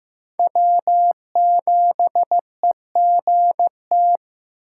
Morse code, heard as a keyed tone, about 18 wpm, 700 Hz. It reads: W7EGT
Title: Speed Racing - USA 1x3 Call Signs 15wpm